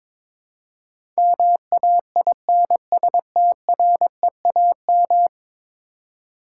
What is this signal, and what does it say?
Morse code, heard as a keyed tone, about 22 wpm, 700 Hz. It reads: MAINSTREAM
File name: practice / 22wpm / 10 Letter Words - First 3 Letters Given - 22wpm